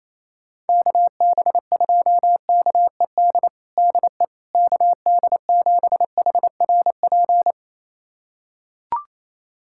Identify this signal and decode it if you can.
Morse code, heard as a keyed tone, about 28 wpm, 700 Hz. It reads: KB2KED DE KD7HRP
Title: USA Callsign DE Callsign 28wpm